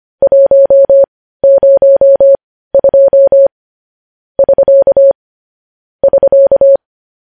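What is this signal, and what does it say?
Morse code, heard as a keyed tone, about 25 wpm, spaced as for 15 wpm, 550 Hz. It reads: 102 <SK> <SK>
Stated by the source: News headlines sent in Morse code at 15 words per minute, preceded by vvv.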